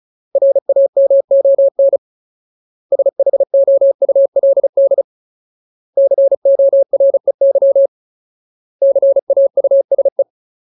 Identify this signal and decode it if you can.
Morse code, heard as a keyed tone, about 34 wpm, 550 Hz. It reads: RAMON SHOULD COREY CAUSE
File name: Ultimate Word List / 35WPM Ultimate